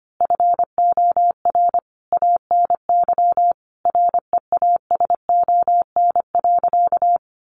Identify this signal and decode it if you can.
Morse code, heard as a keyed tone, about 25 wpm, 700 Hz. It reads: FOR ANY REASON.
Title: A Sound Of Thunder 25wpm